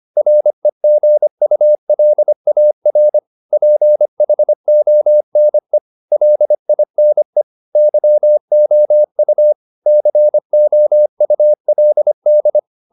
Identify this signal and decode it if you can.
Morse code, heard as a keyed tone, about 25 wpm, 600 Hz. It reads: REGULAR PHONE LINE YOU COULD